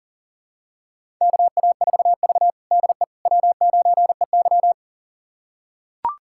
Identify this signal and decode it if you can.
Morse code, heard as a keyed tone, about 40 wpm, 700 Hz. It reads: KA4V DE W9EY
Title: USA Callsign DE Callsign - 4 Characters - 40wpm